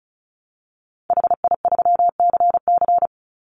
Morse code, heard as a keyed tone, about 34 wpm, 700 Hz. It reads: HI3CC
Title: Call Signs Repeated 3x - 35wpm